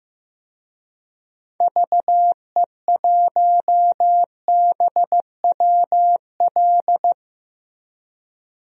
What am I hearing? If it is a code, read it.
Morse code, heard as a keyed tone, about 16 wpm, 700 Hz. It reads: VE1BWL
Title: DXCC-100 Call Signs - 6 Characters - 15wpm